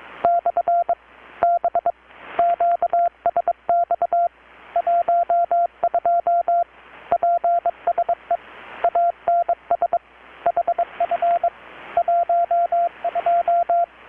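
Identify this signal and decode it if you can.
Morse code, heard as a keyed tone, about 22 wpm, 690 Hz, with radio noise. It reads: / B QSX 12 PSE ANS HF 12
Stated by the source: Morse from KPH's Night of Nights XX maritime radiotelegraphy event, received in Chiapas, Mexico about 3,800 km away on a Kenwood R-600 with a grounded 10 m long-wire antenna.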